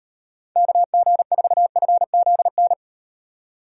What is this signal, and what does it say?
Morse code, heard as a keyed tone, about 38 wpm, 700 Hz. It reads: KG4FZN